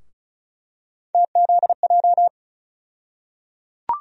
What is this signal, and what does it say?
Morse code, heard as a keyed tone, about 36 wpm, 700 Hz. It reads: TZJ
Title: USA Call Sign Postfixes - 3 Letters 35wpm